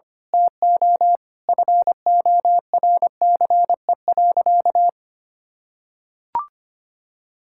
Morse code, heard as a keyed tone, about 25 wpm, 700 Hz. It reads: TO FORCE.
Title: Sentences from Top 1000 Words - Review Part 6 - 25wpm